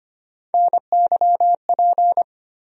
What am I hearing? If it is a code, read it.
Morse code, heard as a keyed tone, about 25 wpm, 700 Hz. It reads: NYP
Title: USA Call Sign Postfixes - 3 Letters 25wpm